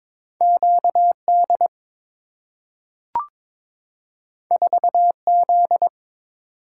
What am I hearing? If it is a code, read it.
Morse code, heard as a keyed tone, about 22 wpm, 700 Hz. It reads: QD 4Z
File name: practice / 22wpm / Two Letter-Number Combinations - 22wpm